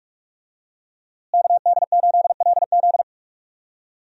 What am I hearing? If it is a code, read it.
Morse code, heard as a keyed tone, about 41 wpm, 700 Hz. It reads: KD8LZ
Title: USA Call Signs - 5 Characters - Repeated 3x - 45wpm